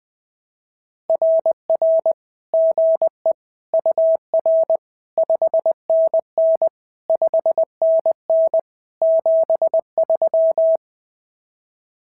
Morse code, heard as a keyed tone, about 20 wpm, 650 Hz. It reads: RR GE UR 5NN 5NN 73
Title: SOTA QSO Practice 20wpm